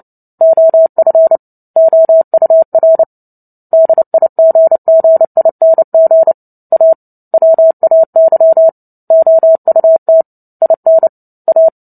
Morse code, heard as a keyed tone, about 30 wpm, 650 Hz. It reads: OF OUR DIGGING A WAY OUT IN A
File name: Book Two / 03-Chapter 3 - 29 WPM